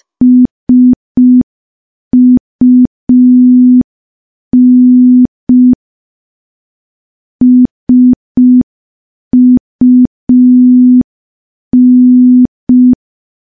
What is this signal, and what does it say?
Morse code, heard as a keyed tone, about 5 wpm, 250 Hz. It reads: SUN SUN
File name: morse / morse-training-05WPM